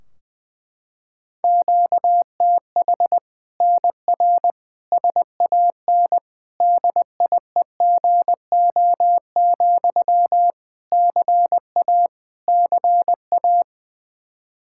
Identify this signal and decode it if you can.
Morse code, heard as a keyed tone, about 20 wpm, 700 Hz. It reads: QTH NR SAN DIEGO, CA CA
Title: Practice QSO - Exchange 1 - Beginning - 20wpm